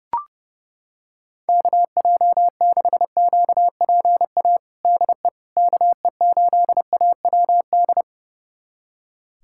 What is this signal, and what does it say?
Morse code, heard as a keyed tone, about 32 wpm, 700 Hz. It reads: KJ6QPA DE KE8AWD
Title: USA Callsign DE Callsign - 6 Characters - 30wpm